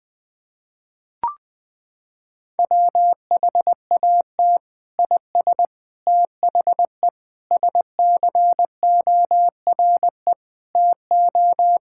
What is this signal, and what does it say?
Morse code, heard as a keyed tone, about 20 wpm, 700 Hz. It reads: WHAT IS THE SCORE TO
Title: Sentences from Top 1600 Words 20wpm